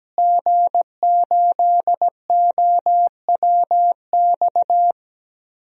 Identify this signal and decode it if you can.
Morse code, heard as a keyed tone, about 17 wpm, 700 Hz. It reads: G8OWX
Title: DXCC-100 Call Signs - 5 Characters - Repeated 3x - 17wpm